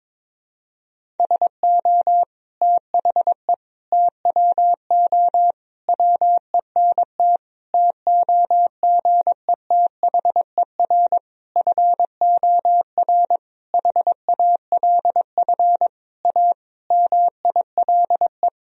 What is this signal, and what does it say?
Morse code, heard as a keyed tone, about 22 wpm, 700 Hz. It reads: SO THE TWO WENT TOGETHER FOR HALF A MILE
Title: Sentences from Top 1000 Words - Review Part 1 - 22wpm